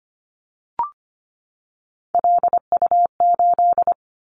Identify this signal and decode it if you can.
Morse code, heard as a keyed tone, about 25 wpm, 700 Hz. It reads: LU8
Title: Callsign Prefixes for Top 50 DXCC - 25wpm